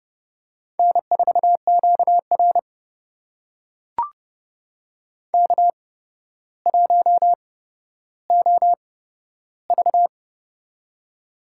Morse code, heard as a keyed tone, about 30 wpm, spaced as for 13 wpm, 700 Hz. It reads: N4QR K1OV
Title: USA Call Signs - 4 Characters - 8x Character Spacing - 30wpm